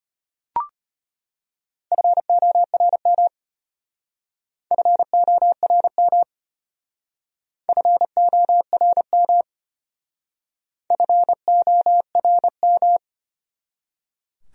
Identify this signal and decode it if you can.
Morse code, heard as a keyed tone, about 32 wpm, 700 Hz. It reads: FORM FORM FORM FORM